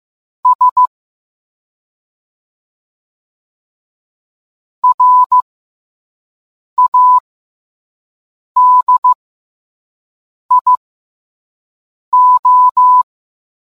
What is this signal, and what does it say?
Morse code, heard as a keyed tone, about 15 wpm, spaced as for 5 wpm, 1000 Hz. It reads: S RADIO